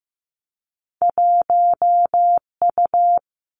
Morse code, heard as a keyed tone, about 15 wpm, 700 Hz. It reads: 1U
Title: Two Challenging Letter-Number Combinations 15wpm